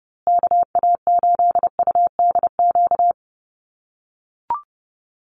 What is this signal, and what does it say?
Morse code, heard as a keyed tone, about 30 wpm, 700 Hz. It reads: KA8UDQ
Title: USA Call Signs Repeated 3x 30wpm